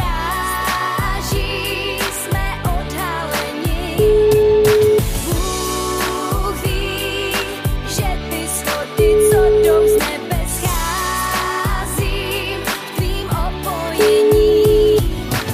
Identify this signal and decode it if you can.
Ringback tone (Europe, 425 Hz), with noise.